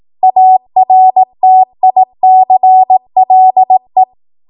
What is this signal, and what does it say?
Morse code, heard as a keyed tone, about 18 wpm, 750 Hz. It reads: ARTICLE